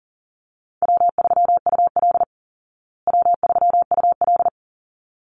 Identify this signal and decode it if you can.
Morse code, heard as a keyed tone, about 40 wpm, 700 Hz. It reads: W3UL W3UL
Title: Easy Call Signs Repeated 2x - 40wpm